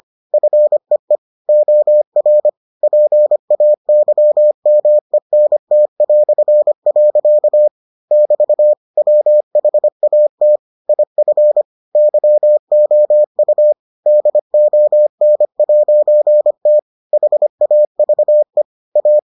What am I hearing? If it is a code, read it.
Morse code, heard as a keyed tone, about 25 wpm, 600 Hz. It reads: FEE OR PAYMENT". <BT> WHAT IF YOU DON'T HAVE A